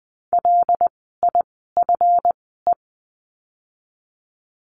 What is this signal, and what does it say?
Morse code, heard as a keyed tone, about 20 wpm, spaced as for 14 wpm, 700 Hz. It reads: LIFE
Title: 237 Lesson 38f - Letter q - Words 20wpm